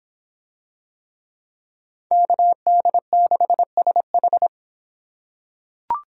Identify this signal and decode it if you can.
Morse code, heard as a keyed tone, about 26 wpm, 700 Hz. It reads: KD6SH